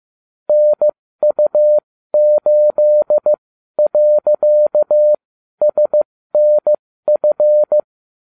Morse code, heard as a keyed tone, about 15 wpm, spaced as for 12 wpm, 600 Hz. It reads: NU8.SNF